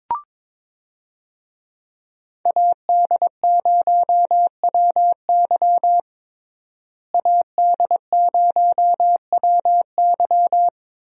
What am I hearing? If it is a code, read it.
Morse code, heard as a keyed tone, about 22 wpm, 700 Hz. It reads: AD0WY AD0WY
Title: USA Call Signs - 5 Characters - Repeated 3x - 22wpm